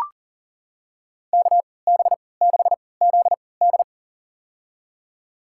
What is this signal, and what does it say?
Morse code, heard as a keyed tone, about 40 wpm, spaced as for 23 wpm, 700 Hz. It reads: KB6ZD
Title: USA Call Signs - 5 Characters - 3x Character Spacing - 40wpm